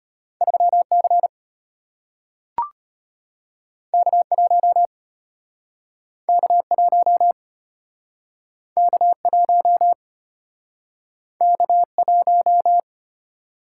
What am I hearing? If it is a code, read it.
Morse code, heard as a keyed tone, about 34 wpm, 700 Hz. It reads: 3C K1 K1 K1 K1